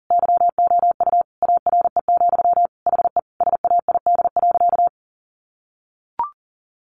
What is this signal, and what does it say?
Morse code, heard as a keyed tone, about 40 wpm, 700 Hz. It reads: YOU ARE, HE SAID.
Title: Sentences from Top 1100 Words 40wpm